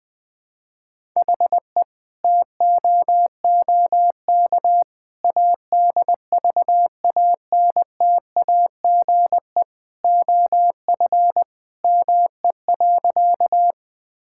Morse code, heard as a keyed tone, about 20 wpm, 700 Hz. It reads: HE TOOK ADVANTAGE OF ME.
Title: Sentences from Top 1000 Words - Review Part 6 - 20wpm